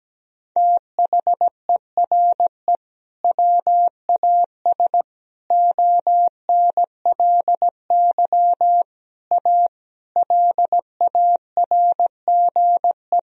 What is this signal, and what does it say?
Morse code, heard as a keyed tone, about 17 wpm, 700 Hz. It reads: THERE WAS ONLY A LARGE
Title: Sentences from Top 1000 Words - Review Part 2 - 17wpm